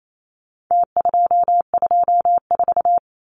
Morse code, heard as a keyed tone, about 28 wpm, 700 Hz. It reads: T224
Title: Numbers - Contest Style - 4-Digit 28wpm